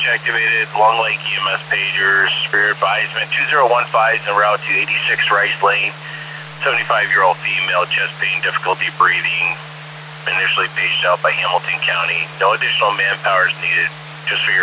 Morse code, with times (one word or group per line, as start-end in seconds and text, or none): none